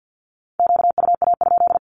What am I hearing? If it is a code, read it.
Morse code, heard as a keyed tone, about 50 wpm, 700 Hz. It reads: QUA?
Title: Q-Codes with Spoken Meaning - 50wpm